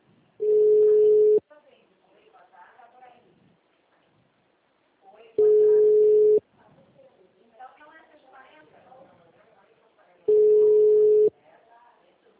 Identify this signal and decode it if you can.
Ringback tone (Europe, 425 Hz), with speech.
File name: out- -2002- - - .42795